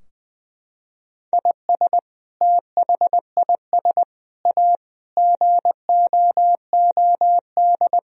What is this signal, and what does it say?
Morse code, heard as a keyed tone, about 20 wpm, 700 Hz. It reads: IS THIS A GOOD
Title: Sentences from Top 1000 Words - Review Part 4 - 20wpm